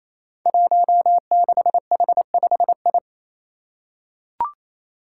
Morse code, heard as a keyed tone, about 28 wpm, 700 Hz. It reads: 16H5I
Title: Five Challenging Letter-Number Combinations 28wpm